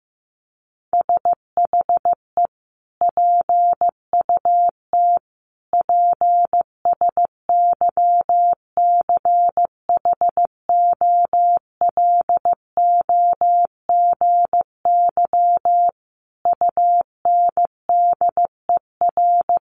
Morse code, heard as a keyed tone, about 15 wpm, 700 Hz. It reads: SHE PUT PSYCHOLOGY UNDER
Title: Sentences from Top 1400 Words 15wpm